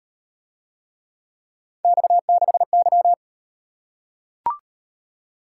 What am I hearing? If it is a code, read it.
Morse code, heard as a keyed tone, about 33 wpm, 700 Hz. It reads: X6Y